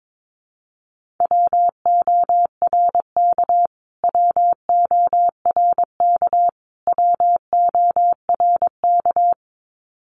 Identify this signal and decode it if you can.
Morse code, heard as a keyed tone, about 22 wpm, 700 Hz. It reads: WORK WORK WORK
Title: Top 200 Words Repeated 22wpm